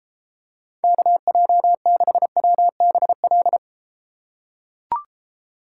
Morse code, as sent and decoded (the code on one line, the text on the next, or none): -.- .--- -.... .-- -... .-..
KJ6WBL